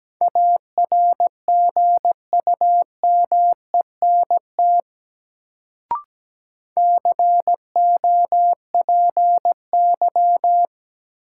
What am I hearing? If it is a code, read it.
Morse code, heard as a keyed tone, about 17 wpm, 700 Hz. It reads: ARGUMENT COPY
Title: Sentences from Top 1400 Words - New Word Drill Alt - 17wpm